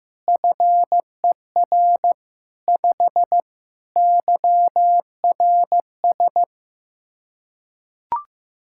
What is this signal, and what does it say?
Morse code, heard as a keyed tone, about 16 wpm, 700 Hz. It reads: FER 5 YRS
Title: Practice QSO - Exchange 3 - More Info - 15wpm